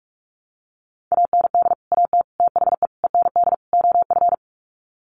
Morse code, heard as a keyed tone, about 44 wpm, 700 Hz. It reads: AND AT THE END OF
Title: Sets of 5 Words 45wpm